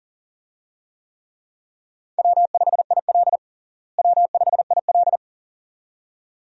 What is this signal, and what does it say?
Morse code, heard as a keyed tone, about 40 wpm, 700 Hz. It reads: W5IL W5IL
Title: USA Call Signs - 4 Characters - Repeated 2x - 40wpm